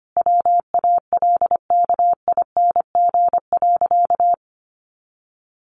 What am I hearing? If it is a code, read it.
Morse code, heard as a keyed tone, about 25 wpm, 700 Hz. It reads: WALKING.
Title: Easy Phrases 25wpm